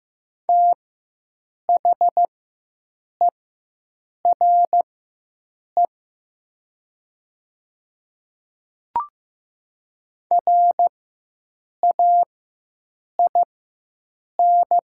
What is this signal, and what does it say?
Morse code, heard as a keyed tone, about 15 wpm, spaced as for 7 wpm, 700 Hz. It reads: THERE RAIN
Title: 4x Character Spacing - Top 300 Words - 15wpm